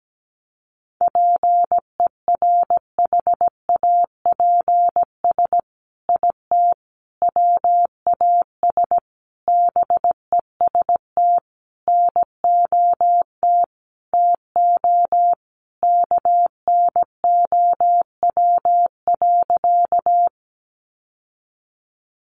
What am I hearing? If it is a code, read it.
Morse code, heard as a keyed tone, about 17 wpm, 700 Hz. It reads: PERHAPS IT WAS BEST NOT TO KNOW.
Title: Sentences from Top 600 Words 17wpm